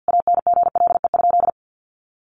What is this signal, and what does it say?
Morse code, heard as a keyed tone, about 50 wpm, 700 Hz. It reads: ANGLE?
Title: Sentences from Top 1500 Words 50wpm